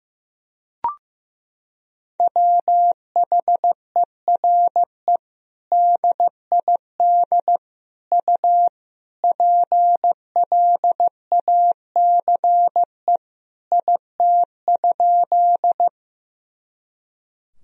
Morse code, heard as a keyed tone, about 16 wpm, 700 Hz. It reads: WHERE DID U PLACE IT?
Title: Sentences from Top 200 Words and Abbreviations 15wpm